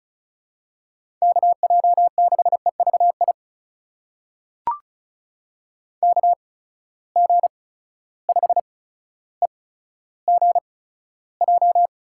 Morse code, heard as a keyed tone, about 36 wpm, spaced as for 15 wpm, 700 Hz. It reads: KJ6EVI KG5EGJ
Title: USA Call Signs - 6 Characters - 8x Character Spacing - 35wpm